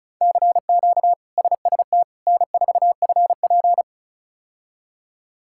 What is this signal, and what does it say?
Morse code, heard as a keyed tone, about 34 wpm, 700 Hz. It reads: CQ SST N4FP
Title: K1USN SST QSO Practice - 35wpm